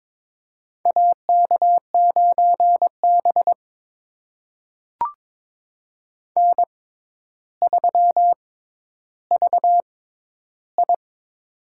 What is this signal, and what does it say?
Morse code, heard as a keyed tone, about 22 wpm, spaced as for 11 wpm, 700 Hz. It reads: AK9B N3VI
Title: USA Call Signs - 4 Characters - 6x Character Spacing - 22wpm